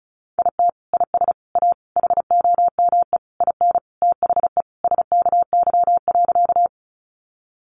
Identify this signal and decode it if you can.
Morse code, heard as a keyed tone, about 34 wpm, 700 Hz. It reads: IT IS A HOME IN THE SKY.
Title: Sentences from Top 1200 Words 35wpm